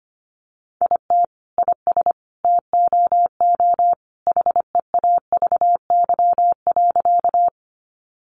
Morse code, heard as a keyed tone, about 25 wpm, 700 Hz. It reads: IT IS TOO HEAVY.
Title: Sentences from Top 1100 Words 25wpm